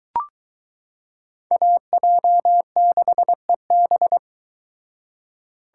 Morse code, heard as a keyed tone, about 22 wpm, 700 Hz. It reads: AJ6EB